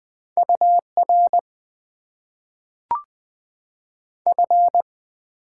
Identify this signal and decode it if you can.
Morse code, heard as a keyed tone, about 20 wpm, 700 Hz. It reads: UR F
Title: Top 90 QSO Elements - 8x Character Spacing - 20wpm